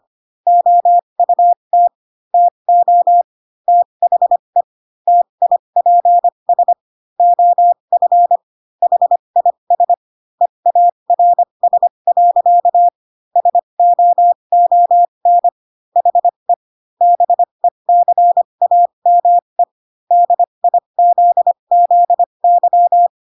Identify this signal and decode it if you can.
Morse code, heard as a keyed tone, about 25 wpm, spaced as for 22 wpm, 700 Hz. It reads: OUT TO THE TIPS OF HIS EARS. SOON HE BECAME DIZZY